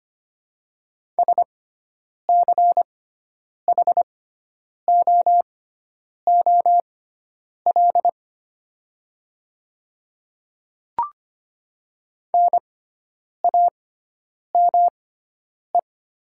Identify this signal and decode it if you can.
Morse code, heard as a keyed tone, about 25 wpm, spaced as for 9 wpm, 700 Hz. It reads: SCHOOL NAME
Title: 6x Character Spacing - Top 500 Words - 25wpm